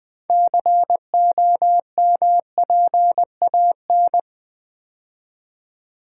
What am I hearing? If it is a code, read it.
Morse code, heard as a keyed tone, about 20 wpm, 700 Hz. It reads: COMPAN
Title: Build A Word - 3 Letters At A Time - 9-Letter Words - 20wpm